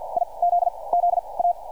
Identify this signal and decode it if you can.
Morse code, heard as a keyed tone, about 51 wpm, 700 Hz. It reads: I Z Z A
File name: archived / CRSnet